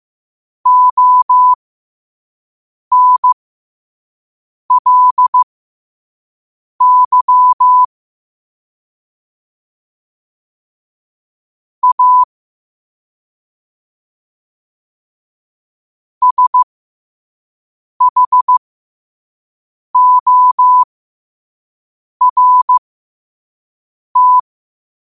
Morse code, heard as a keyed tone, about 15 wpm, spaced as for 5 wpm, 1000 Hz. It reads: ONLY A SHORT
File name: qotdfiles / 5wpm